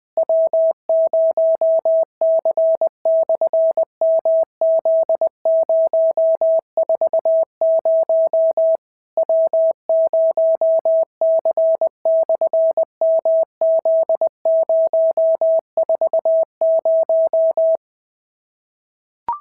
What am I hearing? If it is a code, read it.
Morse code, heard as a keyed tone, about 18 wpm, 650 Hz. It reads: W0C/MZ040 W0C/MZ040
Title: SOTA QSO Practice 20wpm